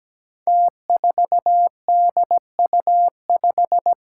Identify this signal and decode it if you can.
Morse code, heard as a keyed tone, about 17 wpm, 700 Hz. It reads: T4DU5
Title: Five Challenging Letter-Number Combinations 17wpm